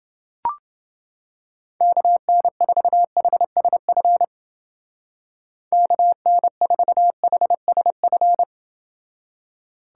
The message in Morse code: KN4HSF KN4HSF

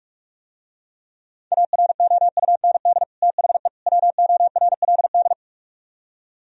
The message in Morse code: AROUND THE WORLD

